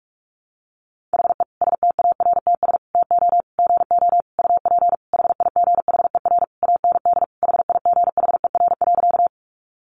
Morse code, heard as a keyed tone, about 44 wpm, 700 Hz. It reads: HE STARTS TO GO UP HIGHER AND HIGHER.
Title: Additional Easy Phrases 45wpm